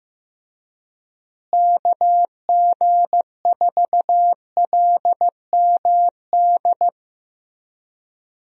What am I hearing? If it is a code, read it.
Morse code, heard as a keyed tone, about 15 wpm, 700 Hz. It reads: KG4LMD
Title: USA Call Signs - 6 Characters - Repeated 2x - 15wpm